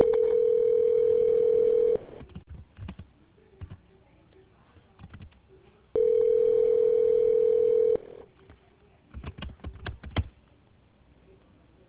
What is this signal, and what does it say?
Ringback tone (North America), 440 + 480 Hz.